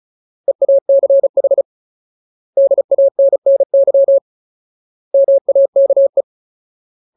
Morse code, heard as a keyed tone, about 34 wpm, 550 Hz. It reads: EACH DANNY MAKE
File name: Ultimate Word List / 35WPM Ultimate